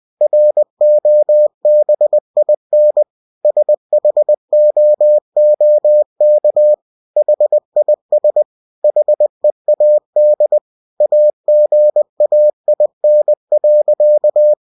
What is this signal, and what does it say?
Morse code, heard as a keyed tone, about 20 wpm, 600 Hz. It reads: ROBIN SHOOK HIS HEAD AGAIN.